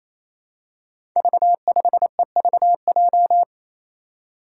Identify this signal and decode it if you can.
Morse code, heard as a keyed tone, about 28 wpm, 700 Hz. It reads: V5EVJ
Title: Five Challenging Letter-Number Combinations 28wpm